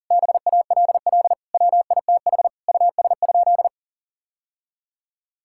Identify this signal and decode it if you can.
Morse code, heard as a keyed tone, about 37 wpm, 700 Hz. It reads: BALL WITH US?
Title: Sentences from Top 1000 Words - Review Part 6 - 40wpm